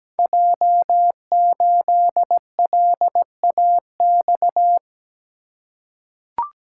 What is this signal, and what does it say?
Morse code, heard as a keyed tone, about 17 wpm, 700 Hz. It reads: J8LAX
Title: Code Groups - 5 Letter:Numbers - 17wpm